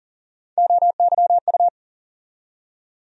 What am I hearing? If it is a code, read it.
Morse code, heard as a keyed tone, about 40 wpm, 700 Hz. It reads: OYU
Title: USA Call Sign Postfixes - 3 Letters 40wpm